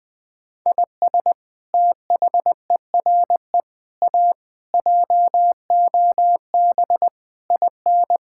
Morse code, heard as a keyed tone, about 20 wpm, 700 Hz. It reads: IS THERE A JOB IN
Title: Sentences from Top 1600 Words 20wpm